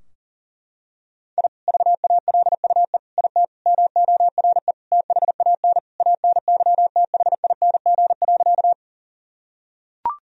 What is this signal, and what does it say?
Morse code, heard as a keyed tone, about 40 wpm, 700 Hz. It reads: I VALUE IT MORE THAN ANYTHING.
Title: Sentences from Top 1000 Words - Review Part 3 - 40wpm